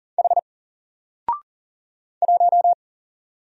Morse code, heard as a keyed tone, about 39 wpm, 700 Hz. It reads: H 1
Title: Single Character (Letter:Number:Punctuation:Prosign) 40wpm